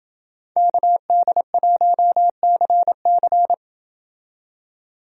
Morse code, heard as a keyed tone, about 29 wpm, 700 Hz. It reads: KD1CC